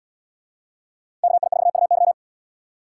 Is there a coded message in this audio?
Yes: Morse code from 1.2 to 2.1 s, heard as a keyed tone, about 77 wpm, 700 Hz. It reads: KI4UY